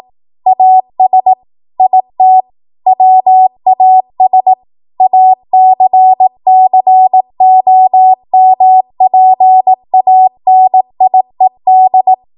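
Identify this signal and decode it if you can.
Morse code, heard as a keyed tone, about 18 wpm, 750 Hz. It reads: AS IT WAS ACCOMPANIED
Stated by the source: ARRL 18 wpm Morse code practice archive transmission.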